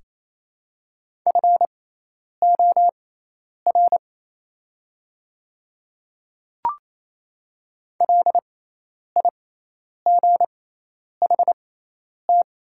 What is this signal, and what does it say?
Morse code, heard as a keyed tone, about 28 wpm, spaced as for 10 wpm, 700 Hz. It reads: FOR LIGHT
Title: 6x Character Spacing - Top 300 Words - 28wpm